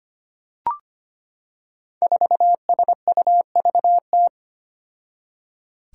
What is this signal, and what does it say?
Morse code, heard as a keyed tone, about 25 wpm, 700 Hz. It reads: 4SUVT
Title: Five Challenging Letter-Number Combinations 25wpm